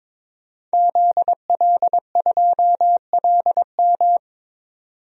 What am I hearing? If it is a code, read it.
Morse code, heard as a keyed tone, about 22 wpm, 700 Hz. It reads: ZL2LM
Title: DXCC-100 Call Signs - 5 Characters - 22wpm